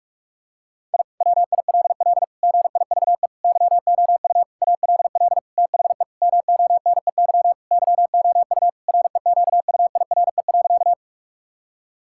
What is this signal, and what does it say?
Morse code, heard as a keyed tone, about 46 wpm, 700 Hz. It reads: I WILL GIVE YOU ALL THE MONEY YOU REQUIRE.
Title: Sentences from Top 1000 Words - Review Part 6 - 45wpm